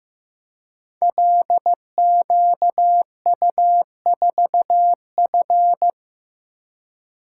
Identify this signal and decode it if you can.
Morse code, heard as a keyed tone, about 15 wpm, 700 Hz. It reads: LQU4F
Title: Code Groups - 5 Letter:Numbers - 15wpm